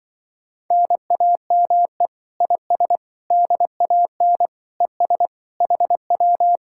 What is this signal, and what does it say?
Morse code, heard as a keyed tone, about 24 wpm, 700 Hz. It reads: NAME IS DAN ES HW
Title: Speed-Racing Common QSO Phrases - 20wpm